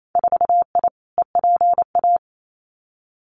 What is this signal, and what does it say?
Morse code, heard as a keyed tone, about 28 wpm, 700 Hz. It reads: 4I EPA
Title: Winter Field Day Exchanges 28wpm